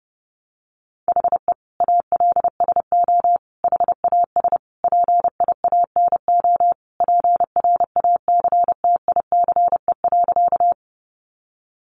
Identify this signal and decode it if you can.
Morse code, heard as a keyed tone, about 30 wpm, 700 Hz. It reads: HE ALSO HAS PIANO PRACTICE.
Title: Additional Easy Phrases 30wpm